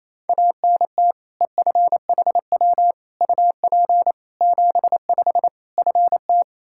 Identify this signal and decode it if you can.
Morse code, heard as a keyed tone, about 27 wpm, 700 Hz. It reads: ANT EFHW UP 75 FT
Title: Practice QSO - Exchange 2 - Info - 28wpm